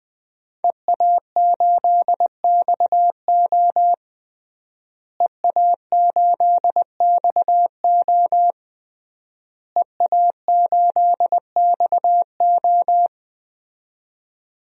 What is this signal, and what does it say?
Morse code, heard as a keyed tone, about 21 wpm, 700 Hz. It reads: EA8XO EA8XO EA8XO
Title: DXCC-100 Call Signs - 5 Characters - Repeated 3x - 20wpm